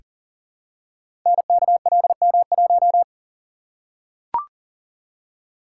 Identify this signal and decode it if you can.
Morse code, heard as a keyed tone, about 39 wpm, 700 Hz. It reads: NKLM1
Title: Code Groups - 5 Letter:Numbers - 40wpm